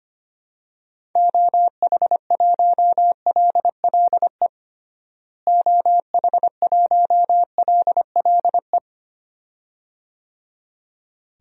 Morse code, heard as a keyed tone, about 25 wpm, 700 Hz. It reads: OH1LLE OH1LLE
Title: DXCC-100 Call Signs - 6 Characters - Repeated 2x - 25wpm